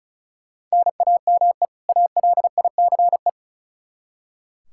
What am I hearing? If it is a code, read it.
Morse code, heard as a keyed tone, about 36 wpm, 700 Hz. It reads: NAME ALICE
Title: Practice QSO - Exchange 1 - Beginning - 35wpm